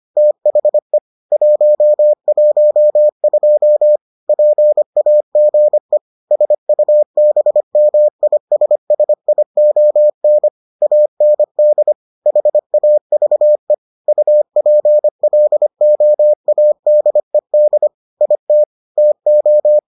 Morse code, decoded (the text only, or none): THE 112 PAGE SUBMISSION AND HAVE UPLOADED IT TO